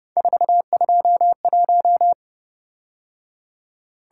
Morse code, heard as a keyed tone, about 29 wpm, 700 Hz. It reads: 421
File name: Speed-Racing - Three Letter-Number Combinations - 25wpm